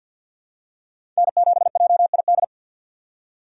Morse code, heard as a keyed tone, about 53 wpm, 700 Hz. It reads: N7JID
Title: USA Call Signs - 5 Characters - 3x Character Spacing - 50wpm